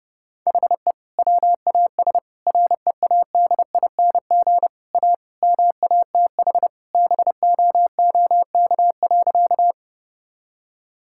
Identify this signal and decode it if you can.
Morse code, heard as a keyed tone, about 30 wpm, 700 Hz. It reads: HE WAS READING A MATH BOOK.
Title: Sentences from Top 1000 Words - Review Part 6 - 30wpm